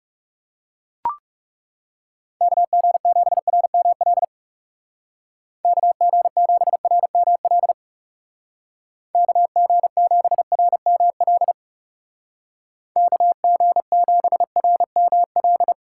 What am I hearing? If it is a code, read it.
Morse code, heard as a keyed tone, about 37 wpm, 700 Hz. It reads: KG7RML KG7RML KG7RML KG7RML
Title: Speed Racing - USA 2x3 Call Signs 30wpm